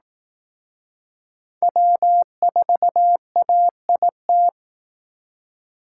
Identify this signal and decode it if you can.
Morse code, heard as a keyed tone, about 18 wpm, 700 Hz. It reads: W4AIT
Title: Speed Racing - USA 1x3 Call Signs 15wpm